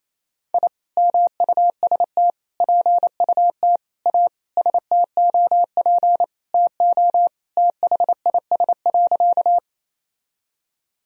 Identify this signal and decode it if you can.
Morse code, heard as a keyed tone, about 28 wpm, 700 Hz. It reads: I MUST PUT A STOP TO THIS.
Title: Sentences from Top 1000 Words - Review Part 1 - 28wpm